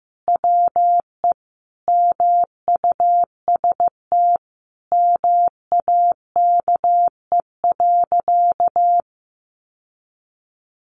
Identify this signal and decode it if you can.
Morse code, heard as a keyed tone, about 15 wpm, 700 Hz. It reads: WE MUST MAKE.
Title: Sentences from Top 1000 Words 15wpm